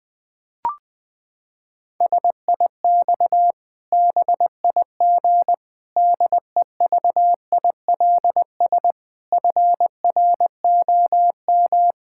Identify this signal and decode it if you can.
Morse code, heard as a keyed tone, about 20 wpm, 700 Hz. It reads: SIX BIG DEVILS FROM
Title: Pangram Challenge v2 - 20wpm